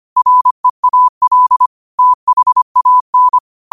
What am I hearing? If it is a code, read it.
Morse code, heard as a keyed tone, about 25 wpm, 1000 Hz. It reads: REAL THAN
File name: qotdfiles / 25wpm